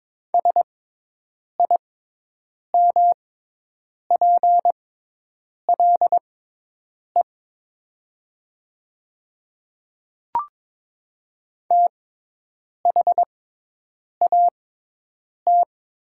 Morse code, heard as a keyed tone, about 22 wpm, spaced as for 8 wpm, 700 Hz. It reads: SIMPLE THAT
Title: 6x Character Spacing - Top 500 Words - 22wpm